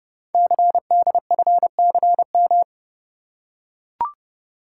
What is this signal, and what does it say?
Morse code, heard as a keyed tone, about 30 wpm, 700 Hz. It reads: CDFCM
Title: Code Groups - 5 Letters - 30wpm